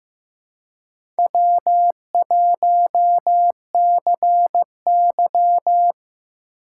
Morse code, heard as a keyed tone, about 15 wpm, 700 Hz. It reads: W1CY
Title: USA Call Signs - 4 Characters - 2x Character Spacing - 15wpm